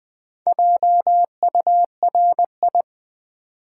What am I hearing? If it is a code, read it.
Morse code, heard as a keyed tone, about 20 wpm, 700 Hz. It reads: JURI
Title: Build A Word - 4 Letters At A Time - 12-Letter Words - 20wpm